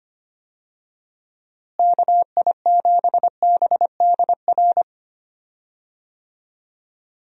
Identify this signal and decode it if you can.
Morse code, heard as a keyed tone, about 25 wpm, 700 Hz. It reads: KI7BDR
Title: USA Call Signs - 6 Characters - Repeated 3x - 25wpm